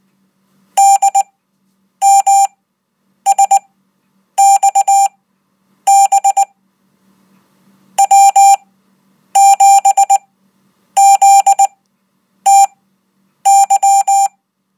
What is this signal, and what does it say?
Morse code, heard as a keyed tone, about 20 wpm, spaced as for 9 wpm, 790 Hz. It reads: DMSXB W7ZTY